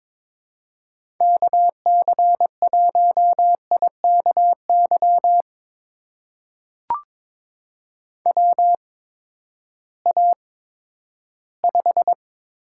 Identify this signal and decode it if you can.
Morse code, heard as a keyed tone, about 22 wpm, spaced as for 12 wpm, 700 Hz. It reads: KC1IKY WA5
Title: USA Call Signs - 6 Characters - 8x Character Spacing - 22wpm